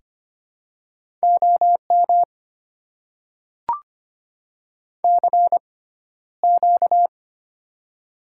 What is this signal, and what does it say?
Morse code, heard as a keyed tone, about 25 wpm, spaced as for 13 wpm, 700 Hz. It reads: OM CQ
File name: Top 90 QSO Elements - 6x Character Spacing - 25wpm